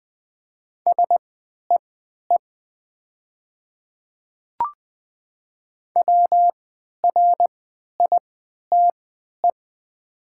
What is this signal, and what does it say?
Morse code, heard as a keyed tone, about 20 wpm, spaced as for 11 wpm, 690 Hz. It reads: SEE WRITE
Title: 3x Character Spacing - Top 100 Words - 20wpm